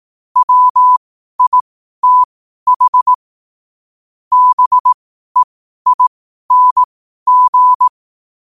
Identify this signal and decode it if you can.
Morse code, heard as a keyed tone, about 18 wpm, spaced as for 12 wpm, 1000 Hz. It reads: WITH BEING
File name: 2026-01-07-12wpm